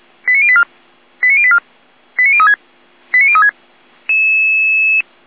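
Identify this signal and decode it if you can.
Five-tone selective call (ZVEI), heard as quick runs of five tones, tones of about 70 ms. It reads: 89084 89084 89036 89036
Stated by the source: ZVEI five-tone selective call received by SDR.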